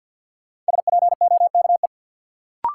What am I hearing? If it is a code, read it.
Morse code, heard as a keyed tone, about 52 wpm, 700 Hz. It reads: SPOKE
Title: Sentences from Top 1200 Words - New Word Drill Alt - 50wpm